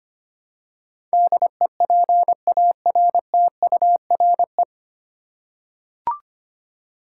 Morse code, heard as a keyed tone, about 25 wpm, 700 Hz. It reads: DEPARTURE
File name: practice / 25wpm / 9 Letter Words - First 3 Letters Given - 25wpm